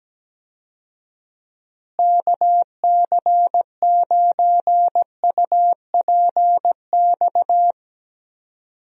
Morse code, heard as a keyed tone, about 17 wpm, 700 Hz. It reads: KC9UPX
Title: USA Call Signs - 6 Characters - Repeated 2x - 17wpm